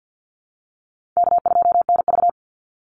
Morse code, heard as a keyed tone, about 50 wpm, 700 Hz. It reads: K2NV
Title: Easy Call Signs Repeated 3x - 50wpm